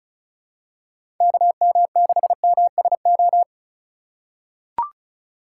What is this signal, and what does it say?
Morse code, heard as a keyed tone, about 34 wpm, 700 Hz. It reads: KM6MSO
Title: USA Call Signs - 6 Characters - 2x Character Spacing - 35wpm